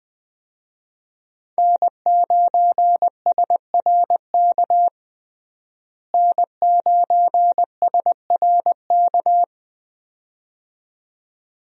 Morse code, heard as a keyed tone, about 20 wpm, 700 Hz. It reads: N9SRK N9SRK
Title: USA Call Signs - 5 Characters - Repeated 2x - 20wpm